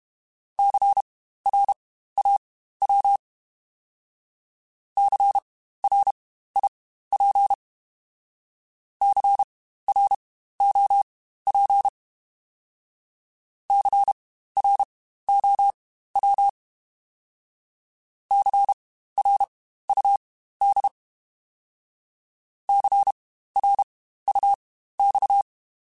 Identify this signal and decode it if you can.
Morse code, heard as a keyed tone, about 32 wpm, spaced as for 15 wpm, 800 Hz. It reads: CRAW CRIP CROP CROW CRUD CRUX
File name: Spaced Alphas / Spaced Alphas mp3s / Cs sp @30